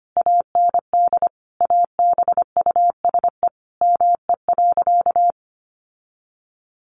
Morse code, heard as a keyed tone, about 25 wpm, 700 Hz. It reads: AND ABUSE ME.
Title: Sentences from Top 1300 Words 25wpm